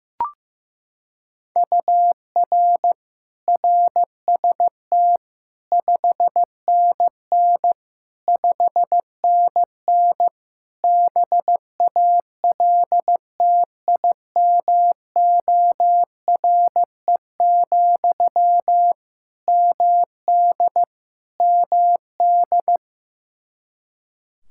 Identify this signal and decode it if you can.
Morse code, heard as a keyed tone, about 15 wpm, 700 Hz. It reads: UR RST 5NN 5NN BALTIMORE, MD MD
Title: Practice QSO - Exchange 1 - Beginning - 15wpm